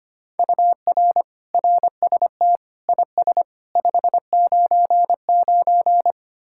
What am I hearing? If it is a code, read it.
Morse code, heard as a keyed tone, about 33 wpm, 700 Hz. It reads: UR RST IS 599